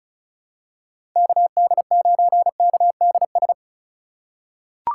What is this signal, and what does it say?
Morse code, heard as a keyed tone, about 36 wpm, 700 Hz. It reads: KD9KDS
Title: USA Call Signs - 6 Characters - 2x Character Spacing - 35wpm